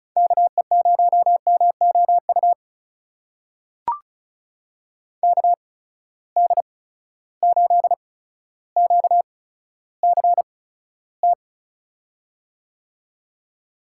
Morse code, heard as a keyed tone, about 36 wpm, spaced as for 15 wpm, 700 Hz. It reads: KE0MOU KD8QCT
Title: USA Call Signs - 6 Characters - 8x Character Spacing - 35wpm